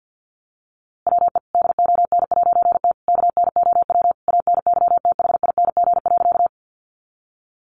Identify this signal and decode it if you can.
Morse code, heard as a keyed tone, about 50 wpm, 700 Hz. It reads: WE DON'T KNOW ANYTHING.
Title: A Sound Of Thunder 50wpm